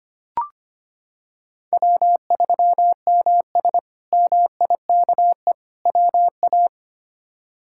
Morse code, heard as a keyed tone, about 25 wpm, 700 Hz. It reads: W3MS MIKE WA
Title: K1USN SST QSO Practice - 25wpm